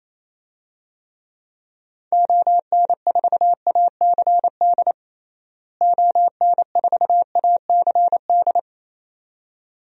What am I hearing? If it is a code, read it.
Morse code, heard as a keyed tone, about 28 wpm, 700 Hz. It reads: ON4ACD ON4ACD
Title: DXCC-100 Call Signs - 6 Characters - Repeated 2x - 28wpm